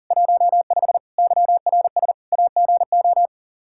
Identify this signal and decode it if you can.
Morse code, heard as a keyed tone, about 40 wpm, 700 Hz. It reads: 15 YRS AGO
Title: Practice QSO - Exchange 3 - More Info - 40wpm